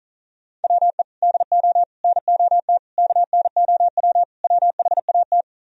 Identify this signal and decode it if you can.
Morse code, heard as a keyed tone, about 46 wpm, 700 Hz. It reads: WE DO NOT KNOW WHAT